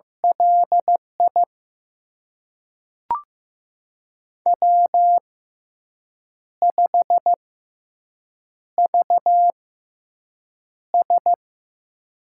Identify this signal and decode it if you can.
Morse code, heard as a keyed tone, about 15 wpm, spaced as for 6 wpm, 700 Hz. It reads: LI W5VS
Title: USA Call Signs - 4 Characters - 6x Character Spacing - 15wpm